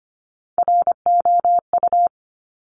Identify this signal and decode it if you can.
Morse code, heard as a keyed tone, about 25 wpm, 700 Hz. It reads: ROU
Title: Top Three Letter Combinations in Words 25wpm